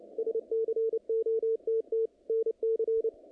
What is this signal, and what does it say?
Morse code, heard as a keyed tone, about 25 wpm, 440 Hz. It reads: SCOTT NC
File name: contests / naqp-cw-k5tr-op-k5pi- -2200